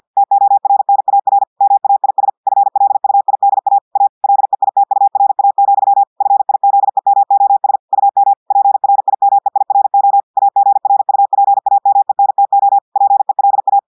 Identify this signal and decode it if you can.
Morse code, heard as a keyed tone, about 50 wpm, 800 Hz. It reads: TORNAR MAIS FLUIDA A LEITURA, FIZEMOS UM PRIMEIRO AGRUPAMENTO PELA
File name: audios / portugais / portugais texte en clair 1 vitesse 50 mots minutes